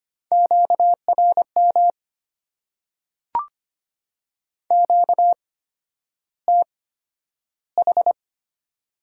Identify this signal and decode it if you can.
Morse code, heard as a keyed tone, about 25 wpm, spaced as for 11 wpm, 700 Hz. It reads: QRM QTH
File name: Top 90 QSO Elements - 8x Character Spacing - 25wpm